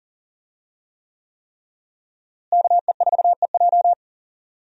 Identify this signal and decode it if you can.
Morse code, heard as a keyed tone, about 40 wpm, 700 Hz. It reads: KE4EJ
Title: USA Call Signs - 5 Characters - Repeated 3x - 40wpm